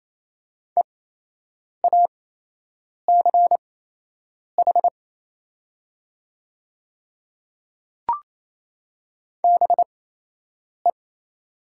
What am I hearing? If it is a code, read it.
Morse code, heard as a keyed tone, about 28 wpm, spaced as for 8 wpm, 700 Hz. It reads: EACH BE
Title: 8x Character Spacing - Top 100 Words - 28wpm